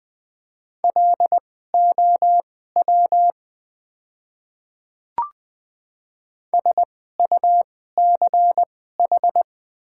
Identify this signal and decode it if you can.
Morse code, heard as a keyed tone, about 20 wpm, spaced as for 14 wpm, 700 Hz. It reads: LOW SUCH
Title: 2x Character Spacing - Top 300 Words - 20wpm